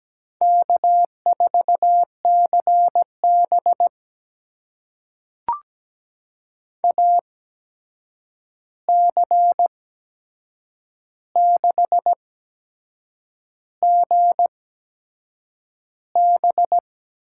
Morse code, heard as a keyed tone, about 17 wpm, spaced as for 7 wpm, 700 Hz. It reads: K4CB AC6GB
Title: USA Call Signs - 8x Character Spacing - 17wpm